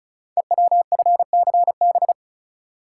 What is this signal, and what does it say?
Morse code, heard as a keyed tone, about 36 wpm, 690 Hz. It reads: EWFCB
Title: Code Groups - 5 Letters - 35wpm